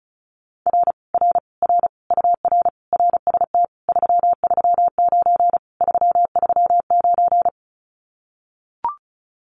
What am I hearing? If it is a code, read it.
Morse code, heard as a keyed tone, about 34 wpm, 700 Hz. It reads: R R R UR RST 339 339
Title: Common QSO Phrases Encore 35wpm